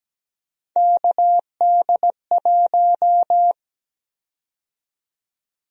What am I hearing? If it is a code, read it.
Morse code, heard as a keyed tone, about 17 wpm, 700 Hz. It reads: KD1
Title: USA Call Sign Prefixes - 17wpm